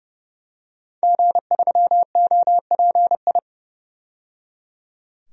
Morse code, heard as a keyed tone, about 30 wpm, 700 Hz. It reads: G3OPI
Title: DXCC-100 Call Signs - 5 Characters - 30wpm